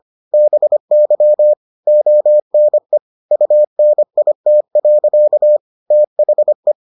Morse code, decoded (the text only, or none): BY ONE UNIT. THE